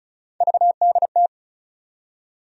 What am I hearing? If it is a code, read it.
Morse code, heard as a keyed tone, about 36 wpm, 700 Hz. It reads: VDT
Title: USA Call Sign Postfixes - 3 Letters 35wpm